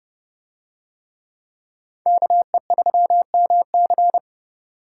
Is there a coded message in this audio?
Yes: Morse code from 2.1 to 4.2 s, heard as a keyed tone, about 29 wpm, 700 Hz. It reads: KE3MC